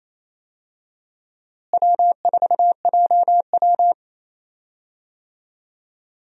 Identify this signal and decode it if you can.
Morse code, heard as a keyed tone, about 28 wpm, 700 Hz. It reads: W4JW
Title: USA Call Signs - 4 Characters - Repeated 3x - 28wpm